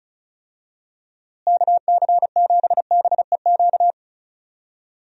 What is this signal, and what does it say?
Morse code, heard as a keyed tone, about 36 wpm, 700 Hz. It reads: KC7BEQ
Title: USA Call Signs - 6 Characters - Repeated 2x - 35wpm